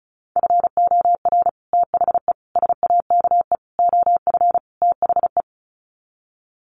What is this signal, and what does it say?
Morse code, heard as a keyed tone, about 34 wpm, 700 Hz. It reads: FOR THE SAKE OF THE
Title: Sets of 5 Words Encore 35wpm